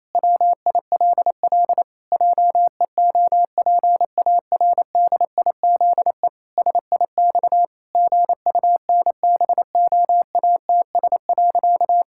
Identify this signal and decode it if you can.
Morse code, heard as a keyed tone, about 28 wpm, 700 Hz. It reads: WILL JEOPARDIZE SIX GUNBOATS.
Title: Pangram Challenge v2 - 28wpm